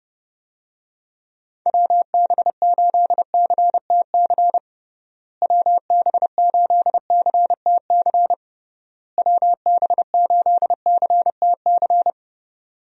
Morse code, heard as a keyed tone, about 32 wpm, 700 Hz. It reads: WB8CTC WB8CTC WB8CTC
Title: USA Call Signs - 6 Characters - Repeated 3x - 30wpm